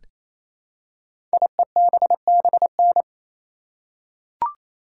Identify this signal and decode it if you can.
Morse code, heard as a keyed tone, about 27 wpm, 700 Hz. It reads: IEBBN
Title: Code Groups - 5 Letters - 28wpm